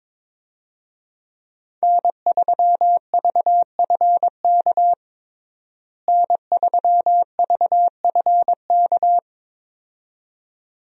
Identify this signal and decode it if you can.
Morse code, heard as a keyed tone, about 22 wpm, 700 Hz. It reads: N3VFK N3VFK
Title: USA Call Signs - 5 Characters - Repeated 2x - 22wpm